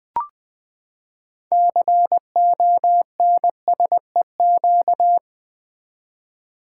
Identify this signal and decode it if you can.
Morse code, heard as a keyed tone, about 20 wpm, 700 Hz. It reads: CONSEQ
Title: Build A Word - 6 Letters At A Time - 12-Letter Words - 20wpm